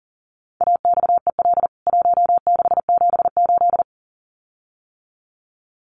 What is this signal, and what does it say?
Morse code, heard as a keyed tone, about 40 wpm, 700 Hz. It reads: AXEL 1678
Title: CWT Exchange 40wpm